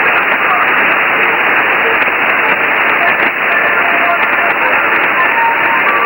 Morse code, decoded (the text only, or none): none